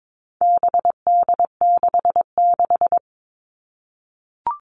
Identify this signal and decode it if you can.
Morse code, heard as a keyed tone, about 22 wpm, 700 Hz. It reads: BD66
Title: All Cut Numbers - 4-Digit 22wpm